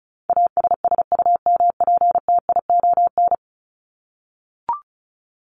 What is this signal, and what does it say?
Morse code, heard as a keyed tone, about 34 wpm, 700 Hz. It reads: ASSUMPTION
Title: Top 1751-2000 Words 35wpm